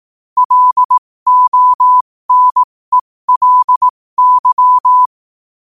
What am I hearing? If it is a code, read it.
Morse code, heard as a keyed tone, about 18 wpm, spaced as for 15 wpm, 1000 Hz. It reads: LONELY
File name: qotdfiles / 15wpm